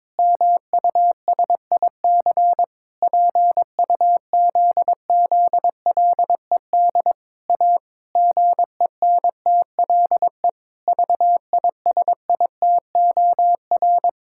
Morse code, heard as a keyed tone, about 22 wpm, 700 Hz. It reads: MUSIC PUZZLED A GENTLE VISITOR
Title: Pangram Challenge v2 - 22wpm